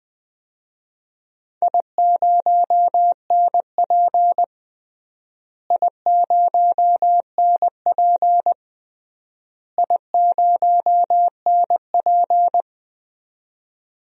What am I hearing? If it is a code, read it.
Morse code, heard as a keyed tone, about 20 wpm, 700 Hz. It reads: I0NP I0NP I0NP
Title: DXCC-100 Call Signs - 4 Characters - Repeated 3x - 20wpm